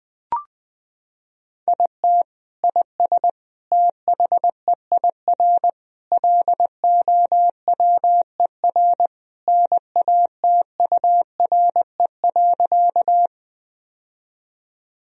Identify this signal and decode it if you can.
Morse code, heard as a keyed tone, about 20 wpm, 700 Hz. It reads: IT IS THEIR LOWER NATURE.
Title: Sentences from Top 1000 Words - Review Part 4 - 20wpm